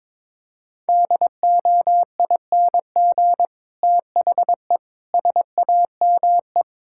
Morse code, heard as a keyed tone, about 22 wpm, 700 Hz. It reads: DOING THE SAME
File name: Sentences with 9-Word Phrases - 22wpm